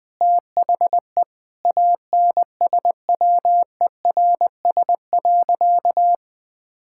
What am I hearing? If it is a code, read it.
Morse code, heard as a keyed tone, about 20 wpm, 700 Hz. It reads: THE ANSWERS.
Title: Sentences with 5-Word Phrases - 20wpm